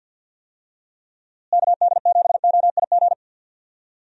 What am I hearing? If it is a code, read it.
Morse code, heard as a keyed tone, about 50 wpm, 700 Hz. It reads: KD7OIG
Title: USA Call Signs - 6 Characters - Repeated 3x - 50wpm